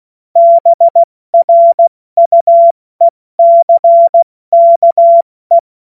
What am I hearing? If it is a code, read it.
Morse code, heard as a keyed tone, about 16 wpm, 680 Hz. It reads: BRUECKE